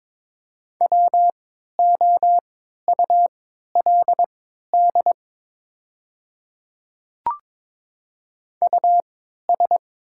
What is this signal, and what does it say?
Morse code, heard as a keyed tone, about 22 wpm, spaced as for 13 wpm, 690 Hz. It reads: WOULD US
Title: 3x Character Spacing - Top 300 Words - 22wpm